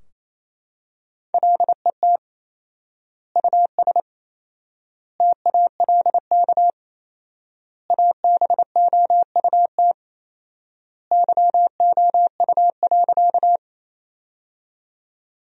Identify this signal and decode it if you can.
Morse code, heard as a keyed tone, about 28 wpm, 700 Hz. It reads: LET US TALK ABOUT YOU.
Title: 4x Word-Spacing - Sentences from Top 400 Words - 28wpm